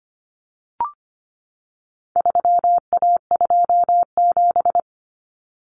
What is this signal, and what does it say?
Morse code, heard as a keyed tone, about 25 wpm, 700 Hz. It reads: 3A27
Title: Common Cut Numbers - 4-Digit 25wpm